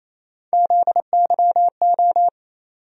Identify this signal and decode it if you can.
Morse code, heard as a keyed tone, about 28 wpm, 700 Hz. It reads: ZYO
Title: USA Call Sign Postfixes - 3 Letters 28wpm